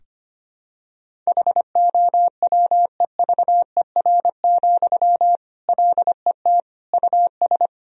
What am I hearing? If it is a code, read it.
Morse code, heard as a keyed tone, about 25 wpm, 700 Hz. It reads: HOWEVER, LET US
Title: Sentences from Top 1000 Words - Review Part 1 - 25wpm